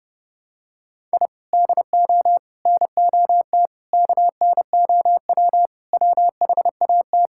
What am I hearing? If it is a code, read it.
Morse code, heard as a keyed tone, about 32 wpm, 700 Hz. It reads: I DO NOT KNOW WHAT
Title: Sentences from Top 1000 Words - Review Part 6 - 30wpm